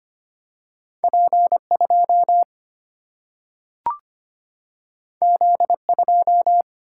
Morse code, heard as a keyed tone, about 25 wpm, 700 Hz. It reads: P2 Z2
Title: Single-Letter Single-Number 25wpm